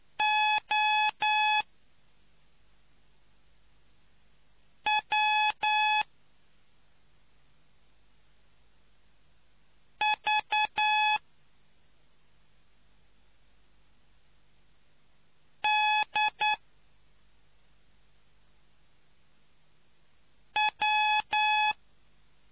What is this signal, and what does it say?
Morse code, heard as a keyed tone, about 9 wpm, spaced as for 2 wpm, 850 Hz. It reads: OW V D W